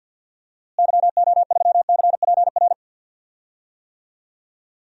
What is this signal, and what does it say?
Morse code, heard as a keyed tone, about 53 wpm, 700 Hz. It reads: YO3CPR
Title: DXCC-100 Call Signs - 6 Characters - Repeated 2x - 50wpm